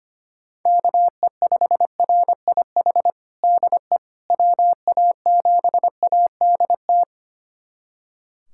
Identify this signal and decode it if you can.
Morse code, heard as a keyed tone, about 25 wpm, 700 Hz. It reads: KE5RIH DE WA7ADT
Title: USA Callsign DE Callsign - 6 Characters - 25wpm